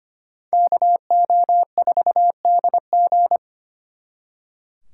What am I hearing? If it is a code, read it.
Morse code, heard as a keyed tone, about 32 wpm, 700 Hz. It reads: KO4DG